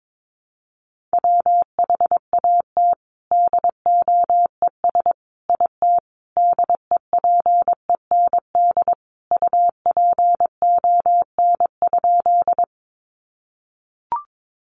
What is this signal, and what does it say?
Morse code, heard as a keyed tone, about 22 wpm, 700 Hz. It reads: WHAT DOES IT DEPEND UPON?
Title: Sentences from Top 1300 Words 22wpm